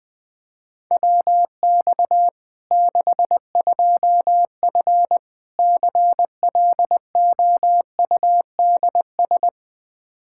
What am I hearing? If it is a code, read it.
Morse code, heard as a keyed tone, about 20 wpm, 700 Hz. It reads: WX 62F CLOUDS
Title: K1USN SST QSO Practice 20wpm